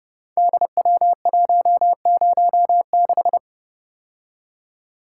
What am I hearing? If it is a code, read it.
Morse code, heard as a keyed tone, about 30 wpm, 700 Hz. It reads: DW106
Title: Code Groups - 5 Letter:Numbers - 30wpm